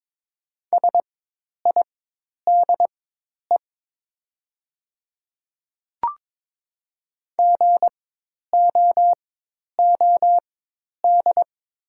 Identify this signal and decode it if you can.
Morse code, heard as a keyed tone, about 22 wpm, spaced as for 10 wpm, 700 Hz. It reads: SIDE GOOD
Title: 4x Character Spacing - Top 200 Words - 22wpm